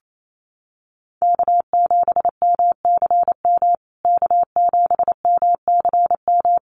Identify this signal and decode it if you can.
Morse code, heard as a keyed tone, about 28 wpm, 700 Hz. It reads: K7MCM K7MCM
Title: USA Call Signs Repeated 3x 28wpm